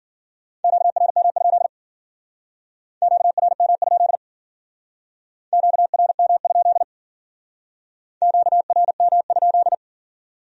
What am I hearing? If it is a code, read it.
Morse code, heard as a keyed tone, about 49 wpm, 700 Hz. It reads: QRM? QRM? QRM? QRM?
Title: Speed-Racing Q-Codes - 40wpm